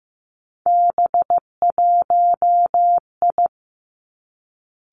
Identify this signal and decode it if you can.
Morse code, heard as a keyed tone, about 15 wpm, 700 Hz. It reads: B1I
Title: Three Challenging Letter-Number Combinations 15wpm